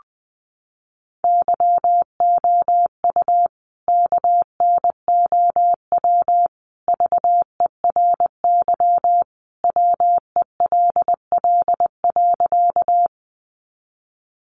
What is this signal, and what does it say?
Morse code, heard as a keyed tone, about 20 wpm, 700 Hz. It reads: YOU KNOW VERY WELL.
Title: A Sound Of Thunder 20wpm